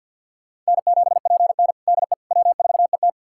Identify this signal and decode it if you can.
Morse code, heard as a keyed tone, about 52 wpm, 700 Hz. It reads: N7PN DE W4ET
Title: USA Callsign DE Callsign - 4 Characters - 50wpm